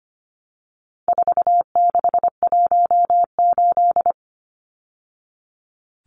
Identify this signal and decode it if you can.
Morse code, heard as a keyed tone, about 25 wpm, 700 Hz. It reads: 4618
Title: Numbers - 4-Digit 25wpm